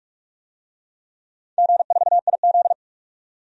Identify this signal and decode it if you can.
Morse code, heard as a keyed tone, about 46 wpm, 700 Hz. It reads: G4IZ
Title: DXCC-100 Call Signs - 4 Characters - 45wpm